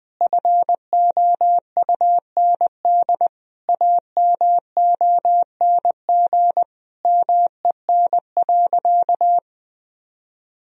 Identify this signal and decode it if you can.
Morse code, heard as a keyed tone, about 20 wpm, 700 Hz. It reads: FOUND AMONG MEN.
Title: Sentences from Top 1000 Words - Review Part 3 - 20wpm